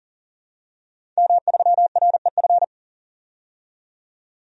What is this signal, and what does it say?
Morse code, heard as a keyed tone, about 40 wpm, 700 Hz. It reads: M3REF
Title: DXCC-100 Call Signs - 5 Characters - 40wpm